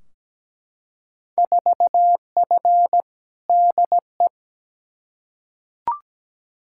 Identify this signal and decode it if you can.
Morse code, heard as a keyed tone, about 17 wpm, 700 Hz. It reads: 4F DE
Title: Field Day Exchanges Encore - 17wpm